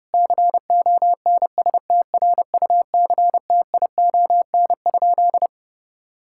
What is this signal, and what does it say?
Morse code, heard as a keyed tone, about 32 wpm, 700 Hz. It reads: CONSTRUCTION?
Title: Sentences from Top 1600 Words 30wpm